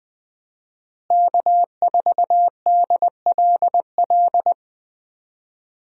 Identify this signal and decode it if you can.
Morse code, heard as a keyed tone, about 20 wpm, 700 Hz. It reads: K4DLL
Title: USA Call Signs - 5 Characters - Repeated 2x - 20wpm